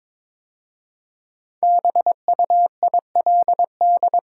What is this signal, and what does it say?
Morse code, heard as a keyed tone, about 22 wpm, 690 Hz. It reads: BUILD
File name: 5 Letter Words - First 3 Letters Given - 22wpm